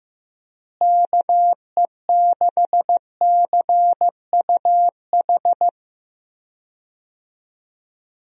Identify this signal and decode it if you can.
Morse code, heard as a keyed tone, about 16 wpm, 700 Hz. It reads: KE6CUH
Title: USA Call Signs - 6 Characters - Repeated 3x - 15wpm